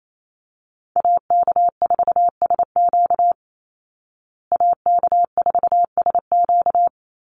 Morse code, heard as a keyed tone, about 28 wpm, 700 Hz. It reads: AK4SQ AK4SQ
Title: Call Signs Repeated 3x - 28wpm